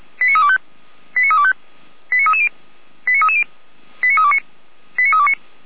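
Five-tone selective call (ZVEI), heard as quick runs of five tones, tones of about 70 ms. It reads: 89325 89325 89330 89330 89329 89329
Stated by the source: Five-tone ZVEI selective call, code 89501, received by SDR.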